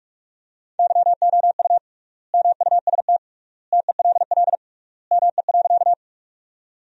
Morse code, heard as a keyed tone, about 46 wpm, 700 Hz. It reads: YOU MUST TELL ME.
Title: 3x Word-Spacing - Sentences from Top 200 Words - 45wpm